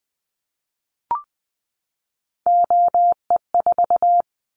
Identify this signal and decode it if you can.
Morse code, heard as a keyed tone, about 20 wpm, 700 Hz. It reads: OE4
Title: Callsign Prefixes for Top 25 DXCC - 20wpm